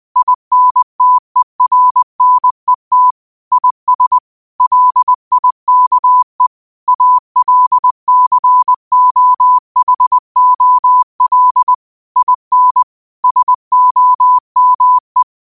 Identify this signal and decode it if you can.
Morse code, heard as a keyed tone, about 21 wpm, 1000 Hz. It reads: INTERNET IS LIKE ALCOHOL IN SOME
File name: 20wpm